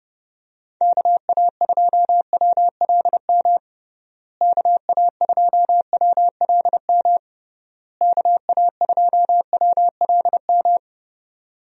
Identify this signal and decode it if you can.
Morse code, heard as a keyed tone, about 32 wpm, 700 Hz. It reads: KA2WLM KA2WLM KA2WLM
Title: USA Call Signs - 6 Characters - Repeated 3x - 30wpm